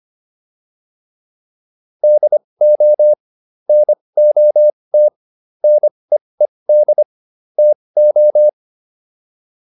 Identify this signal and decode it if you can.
Morse code, heard as a keyed tone, about 25 wpm, spaced as for 20 wpm, 600 Hz. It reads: DO NOT NEED TO